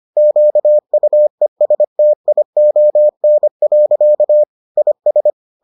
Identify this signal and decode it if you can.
Morse code, heard as a keyed tone, about 25 wpm, 600 Hz. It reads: QUESTION. IS